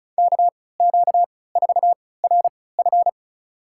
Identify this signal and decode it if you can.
Morse code, heard as a keyed tone, about 36 wpm, spaced as for 20 wpm, 690 Hz. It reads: KQ4RF
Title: USA Call Signs - 5 Characters - 3x Character Spacing - 35wpm